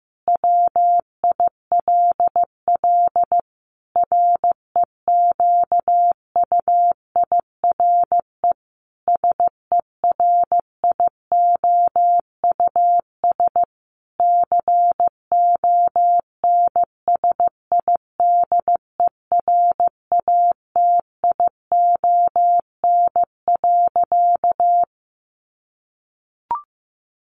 Morse code, heard as a keyed tone, about 15 wpm, 700 Hz. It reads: WILL REQUIRE SERIOUS CONSIDERATION.
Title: Sentences from Top 1500 Words 15wpm